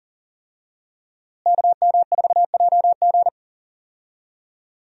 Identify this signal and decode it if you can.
Morse code, heard as a keyed tone, about 40 wpm, 700 Hz. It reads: KM4JG
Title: USA Call Signs - 5 Characters - Repeated 2x - 40wpm